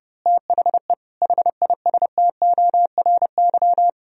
Morse code, heard as a keyed tone, about 30 wpm, 700 Hz. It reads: THE HISTORY
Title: Speed-Racing - Sets of 4 Words 30wpm